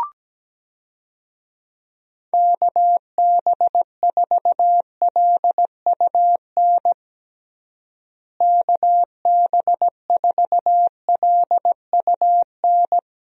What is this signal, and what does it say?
Morse code, heard as a keyed tone, about 17 wpm, 700 Hz. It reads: KB4LUN KB4LUN
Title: USA Call Signs - 6 Characters - Repeated 2x - 17wpm